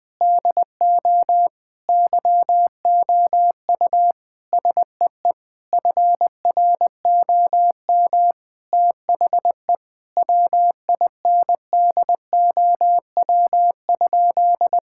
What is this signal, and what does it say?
Morse code, heard as a keyed tone, about 20 wpm, 700 Hz. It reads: DO YOU SEE FROM THE WINDOW?
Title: Sentences from Top 1600 Words 20wpm